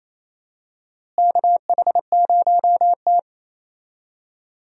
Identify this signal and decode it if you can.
Morse code, heard as a keyed tone, about 28 wpm, 700 Hz. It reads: KH0T
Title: USA Call Signs - 4 Characters - Repeated 3x - 28wpm